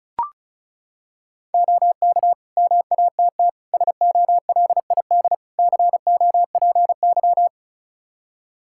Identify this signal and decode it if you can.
Morse code, heard as a keyed tone, about 36 wpm, 700 Hz. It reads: OK MATT SOLID COPY
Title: Practice QSO - Exchange 3 - More Info - 35wpm